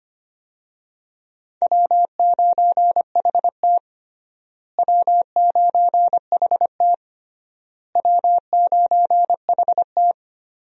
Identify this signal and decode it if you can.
Morse code, heard as a keyed tone, about 25 wpm, 700 Hz. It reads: W9HT W9HT W9HT
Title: USA Call Signs - 4 Characters - Repeated 3x - 25wpm